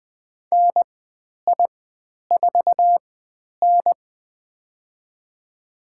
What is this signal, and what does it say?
Morse code, heard as a keyed tone, about 20 wpm, spaced as for 10 wpm, 700 Hz. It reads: NI4N
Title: 095 Lesson 16g - Letter c - Call Signs 20wpm